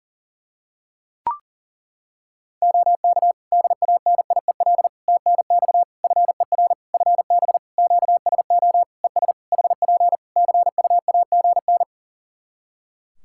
Morse code, heard as a keyed tone, about 40 wpm, 700 Hz. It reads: OK DANIEL TNX FER FB QSO ES HP CUAGN
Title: Practice QSO - Exchange 4 - Ending - 40wpm